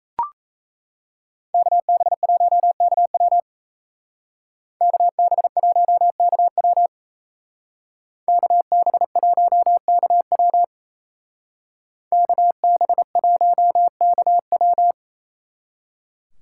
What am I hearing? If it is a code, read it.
Morse code, heard as a keyed tone, about 36 wpm, 700 Hz. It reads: KB1KW KB1KW KB1KW KB1KW